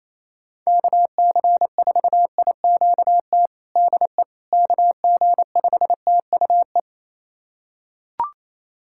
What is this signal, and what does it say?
Morse code, heard as a keyed tone, about 28 wpm, 700 Hz. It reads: KC4IQT DE KG5TUE
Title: USA Callsign DE Callsign 28wpm